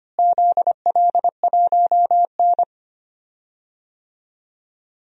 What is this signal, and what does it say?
Morse code, heard as a keyed tone, about 25 wpm, 700 Hz. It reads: ZL1N
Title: DXCC-100 Call Signs - 4 Characters - Repeated 3x - 25wpm